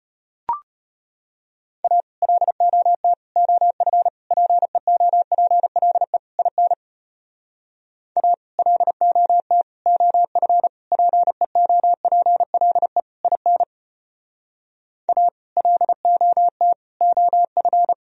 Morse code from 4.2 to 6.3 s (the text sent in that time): PEOPLE